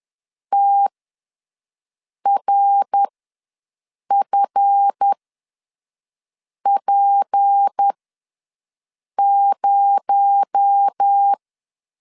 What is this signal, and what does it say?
Morse code, heard as a keyed tone, about 10 wpm, spaced as for 5 wpm, 800 Hz. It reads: TRFP0
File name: LEZIONI / 7 GF9Z / 7 GF9Z ws 10wpm 3919